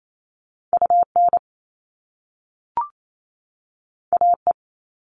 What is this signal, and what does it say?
Morse code, heard as a keyed tone, about 28 wpm, 700 Hz. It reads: UN AE
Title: All Cut Numbers - 2-Digit 28wpm